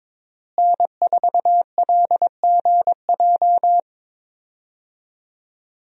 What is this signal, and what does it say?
Morse code, heard as a keyed tone, about 22 wpm, 700 Hz. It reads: N4LGJ
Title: USA Call Signs - 5 Characters - Repeated 3x - 22wpm